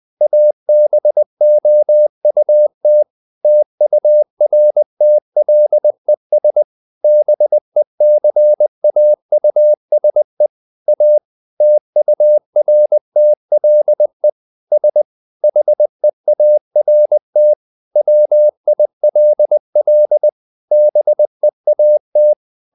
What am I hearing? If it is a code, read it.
Morse code, heard as a keyed tone, about 20 wpm, 600 Hz. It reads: ABOUT TURTLES BECAUSE A TURTLE S HEART WILL BEAT
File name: the old man and the sea / the old man and the sea part 1 20wpm